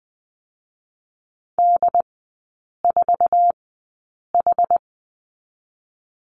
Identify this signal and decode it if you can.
Morse code, heard as a keyed tone, about 20 wpm, 700 Hz. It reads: D 4 H
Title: 184 Lesson 30e - Letter b - Three Characters 20wpm